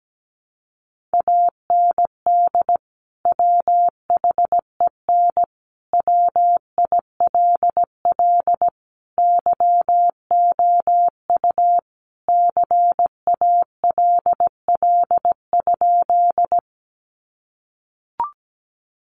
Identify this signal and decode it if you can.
Morse code, heard as a keyed tone, about 17 wpm, 700 Hz. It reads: AND WHEN WILL YOU CALL?
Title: Sentences from Top 100 Words 17wpm